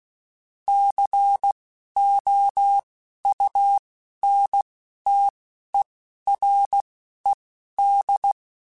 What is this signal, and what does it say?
Morse code, heard as a keyed tone, about 16 wpm, spaced as for 11 wpm, 800 Hz. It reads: COUNTERED
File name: Technical Subjects / Technical Subjects-mp3s / XC @10